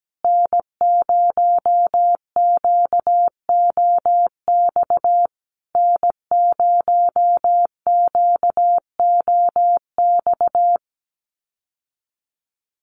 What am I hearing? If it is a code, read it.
Morse code, heard as a keyed tone, about 17 wpm, 700 Hz. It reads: N0QOX N0QOX
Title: USA Call Signs Repeated 3x 17wpm